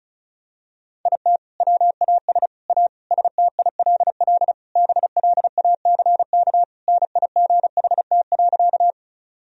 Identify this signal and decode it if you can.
Morse code, heard as a keyed tone, about 34 wpm, 700 Hz. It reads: IT WAS A STILL BLACK NIGHT.
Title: Sentences from Top 1000 Words - Review Part 2 - 35wpm